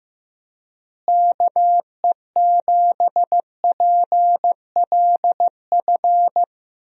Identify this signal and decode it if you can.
Morse code, heard as a keyed tone, about 15 wpm, 700 Hz. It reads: KE7PLF
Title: USA Call Signs - 6 Characters - Repeated 3x - 15wpm